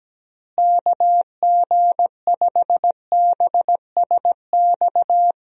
Morse code, heard as a keyed tone, about 17 wpm, 700 Hz. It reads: KG5BSX
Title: USA Call Signs - 6 Characters - Repeated 3x - 17wpm